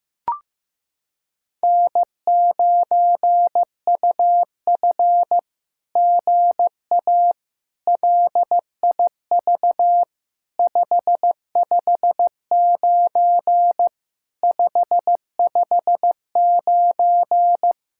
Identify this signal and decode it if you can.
Morse code, heard as a keyed tone, about 15 wpm, 700 Hz. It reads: N9UF GA LIV 559 559
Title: SOTA QSO Practice 15wpm